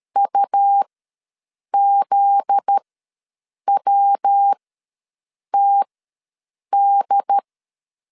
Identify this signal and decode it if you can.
Morse code, heard as a keyed tone, about 13 wpm, spaced as for 7 wpm, 800 Hz. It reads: UZWTD
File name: LEZIONI / 8 YJ45 / 8 YJ ws 12wpm 3342